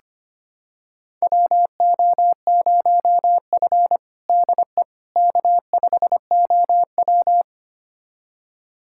Morse code, heard as a keyed tone, about 25 wpm, 700 Hz. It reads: WO0F DE K5OW
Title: USA Callsign DE Callsign - 4 Characters - 25wpm